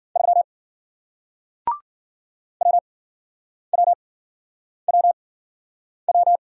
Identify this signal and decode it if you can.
Morse code, heard as a keyed tone, about 56 wpm, spaced as for 9 wpm, 700 Hz. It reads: 3 WWWW